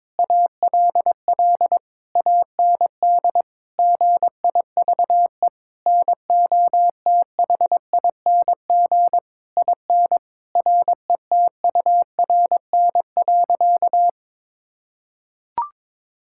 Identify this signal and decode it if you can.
Morse code, heard as a keyed tone, about 22 wpm, 700 Hz. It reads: ALL AND GIVE NOTHING IN RETURN.
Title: Sentences from Top 1000 Words - Review Part 6 - 22wpm